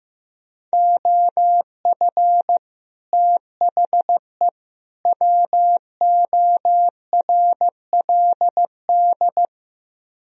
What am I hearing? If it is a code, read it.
Morse code, heard as a keyed tone, about 15 wpm, 700 Hz. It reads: OF THE WORLD
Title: Speed-Racing - Sets of 3 Words 15wpm